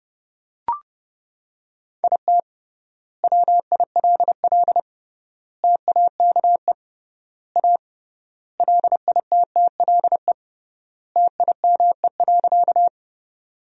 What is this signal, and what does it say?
Morse code, heard as a keyed tone, about 32 wpm, 700 Hz. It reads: IT WILL TAKE A LITTLE TIME.
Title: 3x Word-Spacing - Sentences from Top 200 Words - 30wpm